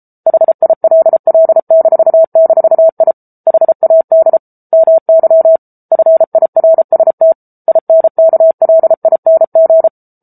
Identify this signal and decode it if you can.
Morse code, heard as a keyed tone, about 32 wpm, 650 Hz. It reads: HILL--I HAD MY FIRST INKLING
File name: Book Two / 07-Chapter 7 - 33 WPM